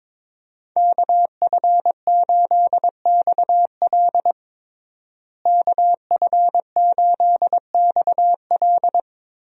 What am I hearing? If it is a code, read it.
Morse code, heard as a keyed tone, about 22 wpm, 700 Hz. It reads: KF8XL KF8XL
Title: USA Call Signs - 5 Characters - Repeated 2x - 22wpm